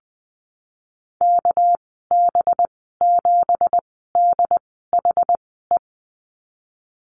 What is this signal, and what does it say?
Morse code, heard as a keyed tone, about 20 wpm, spaced as for 15 wpm, 700 Hz. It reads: KB7DHE
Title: 219 Lesson 35g - Number 0 - Call Signs 20wpm